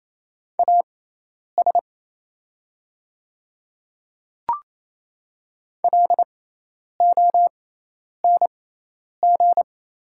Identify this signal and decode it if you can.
Morse code, heard as a keyed tone, about 28 wpm, spaced as for 10 wpm, 700 Hz. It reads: AS LONG
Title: 6x Character Spacing - Top 100 Words - 28wpm